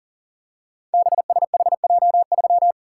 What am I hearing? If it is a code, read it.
Morse code, heard as a keyed tone, about 42 wpm, 690 Hz. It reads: BSHJ3
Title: Code Groups - 5 Letter:Numbers - 40wpm